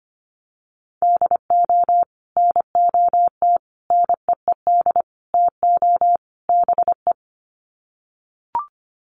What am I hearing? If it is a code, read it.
Morse code, heard as a keyed tone, about 25 wpm, 700 Hz. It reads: DO NOT NEED TO BE
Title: Sets of 5 Words Encore 25wpm